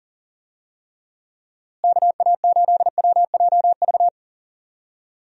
Morse code, heard as a keyed tone, about 40 wpm, 700 Hz. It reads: KA8WJV
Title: USA Call Signs - 6 Characters - Repeated 2x - 40wpm